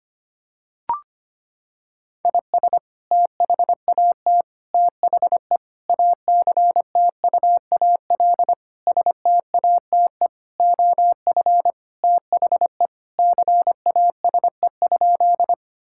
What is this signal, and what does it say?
Morse code, heard as a keyed tone, about 25 wpm, 700 Hz. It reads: IS THAT THE ACTUAL STATE OF THE CASE?
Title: Sentences from Top 1000 Words - Review Part 5 - 25wpm